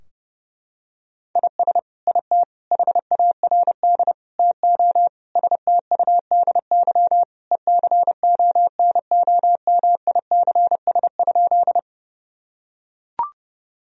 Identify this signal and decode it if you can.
Morse code, heard as a keyed tone, about 30 wpm, 700 Hz. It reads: IS IT HARD TO STUDY ECONOMICS?
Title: Sentences from Top 1000 Words - Review Part 3 - 30wpm